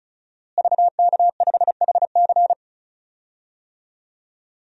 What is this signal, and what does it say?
Morse code, heard as a keyed tone, about 36 wpm, 700 Hz. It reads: VK5HC
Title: DXCC-100 Call Signs - 5 Characters - Repeated 3x - 35wpm